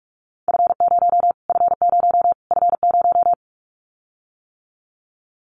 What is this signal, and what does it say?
Morse code, heard as a keyed tone, about 44 wpm, 700 Hz. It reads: F0 F0 F0
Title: Callsign Prefixes for Top 25 DXCC - Repeated 3x - 45wpm